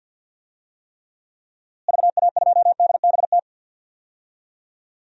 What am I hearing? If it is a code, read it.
Morse code, heard as a keyed tone, about 46 wpm, 700 Hz. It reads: VA2DBT
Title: DXCC-100 Call Signs - 6 Characters - Repeated 2x - 50wpm